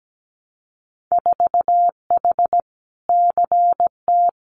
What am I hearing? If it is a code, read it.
Morse code, heard as a keyed tone, about 17 wpm, 700 Hz. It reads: 4H CT
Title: Winter Field Day Exchanges Encore 17wpm